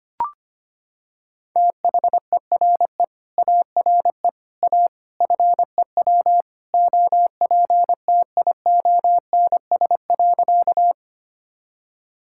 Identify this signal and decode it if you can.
Morse code, heard as a keyed tone, about 25 wpm, 700 Hz. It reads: THERE ARE A FEW OPTIONS.
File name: practice / 25wpm / Sentences with 3-Word Phrases - 25wpm